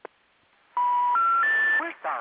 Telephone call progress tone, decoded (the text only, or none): special information tone